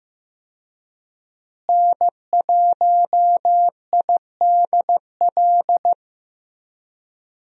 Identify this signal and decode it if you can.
Morse code, heard as a keyed tone, about 16 wpm, 700 Hz. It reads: N1IDL
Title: USA Call Signs - 5 Characters - Repeated 2x - 15wpm